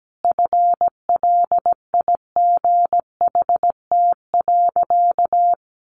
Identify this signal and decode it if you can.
Morse code, heard as a keyed tone, about 17 wpm, 700 Hz. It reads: FLIGHT.
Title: Sentences from Top 1400 Words 17wpm